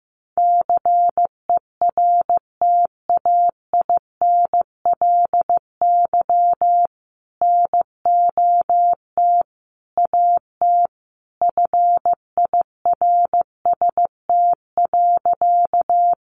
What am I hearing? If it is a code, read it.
Morse code, heard as a keyed tone, about 14 wpm, 700 Hz. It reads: CERTAINLY NOT AT FIRST.
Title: Sentences from Top 900 Words 15wpm